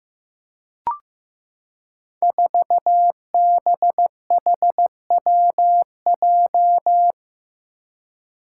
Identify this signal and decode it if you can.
Morse code, heard as a keyed tone, about 15 wpm, 700 Hz. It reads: 4BHWJ
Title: Five Challenging Letter-Number Combinations 15wpm